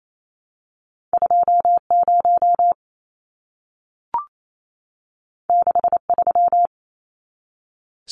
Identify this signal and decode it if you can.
Morse code, heard as a keyed tone, about 28 wpm, 700 Hz. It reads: 20 63
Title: Numbers - 2-Digit 28wpm